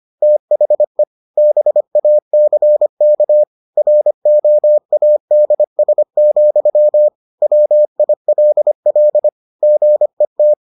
Morse code, heard as a keyed tone, about 25 wpm, 600 Hz. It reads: THE BACK ROADS, WILL GET